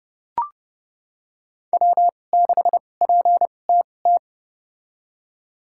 Morse code, heard as a keyed tone, about 30 wpm, spaced as for 22 wpm, 700 Hz. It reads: W6PTT
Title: USA Call Signs - 5 Characters - 2x Character Spacing - 30wpm